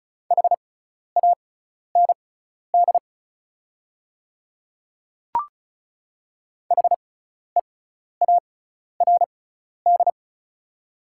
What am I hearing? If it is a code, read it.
Morse code, heard as a keyed tone, about 34 wpm, spaced as for 12 wpm, 690 Hz. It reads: HAND HEARD
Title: 6x Character Spacing - Top 500 Words - 35wpm